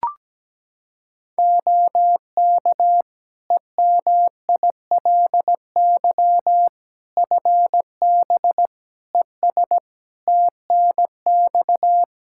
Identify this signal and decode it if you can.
Morse code, heard as a keyed tone, about 17 wpm, 700 Hz. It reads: OK EMILY FB ES TNX
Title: Practice QSO - Exchange 2 - Info - 17wpm